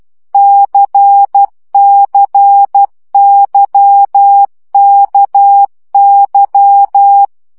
Morse code, heard as a keyed tone, about 12 wpm, 800 Hz. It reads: CCYKY
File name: CW gyakorlas k c y 10 baud